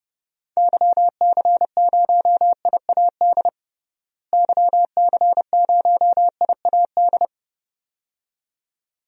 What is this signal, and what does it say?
Morse code, heard as a keyed tone, about 32 wpm, 700 Hz. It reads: YC0IAD YC0IAD
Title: DXCC-100 Call Signs - 6 Characters - Repeated 2x - 30wpm